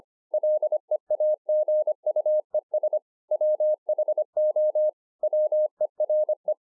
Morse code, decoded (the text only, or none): LEAGUES WHO WERE